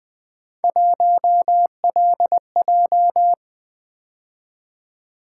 Morse code, heard as a keyed tone, about 21 wpm, 700 Hz. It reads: 1LJ
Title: Three Letter-Number Combinations - 20wpm